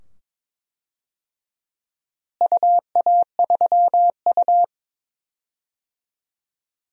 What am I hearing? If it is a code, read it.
Morse code, heard as a keyed tone, about 22 wpm, 700 Hz. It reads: UA3U
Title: DXCC-100 Call Signs - 4 Characters - Repeated 2x - 22wpm